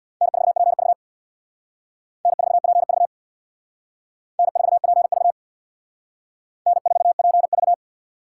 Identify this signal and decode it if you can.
Morse code, heard as a keyed tone, about 58 wpm, 700 Hz. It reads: N4PV N4PV N4PV N4PV
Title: Speed Racing - USA 1x2 Call Signs 50wpm